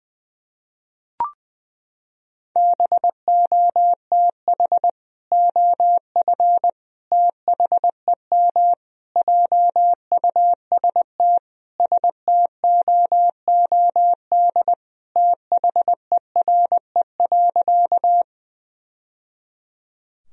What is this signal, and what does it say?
Morse code, heard as a keyed tone, about 21 wpm, 700 Hz. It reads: BOTH OF THEM JUST STOOD THERE.
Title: Sentences from Top 1000 Words - Review Part 2 - 20wpm